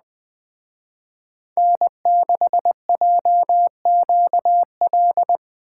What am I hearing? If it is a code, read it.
Morse code, heard as a keyed tone, about 19 wpm, 700 Hz. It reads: N6JQL